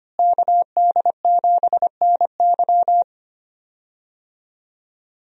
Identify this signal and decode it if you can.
Morse code, heard as a keyed tone, about 25 wpm, 700 Hz. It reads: KD7NY
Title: USA Call Signs - 5 Characters - Repeated 3x - 25wpm